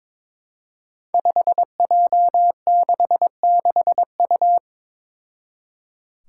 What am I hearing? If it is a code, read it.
Morse code, heard as a keyed tone, about 22 wpm, 700 Hz. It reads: 5J66U
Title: Five Challenging Letter-Number Combinations 22wpm